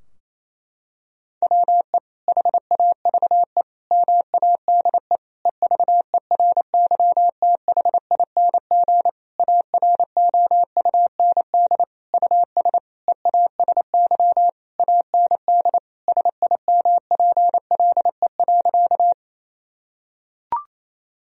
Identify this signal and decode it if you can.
Morse code, heard as a keyed tone, about 28 wpm, 700 Hz. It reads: WE HAVE MADE EVERYTHING AROUND US EASY AND SIMPLE.
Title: Sentences from Top 1000 Words - Review Part 3 - 28wpm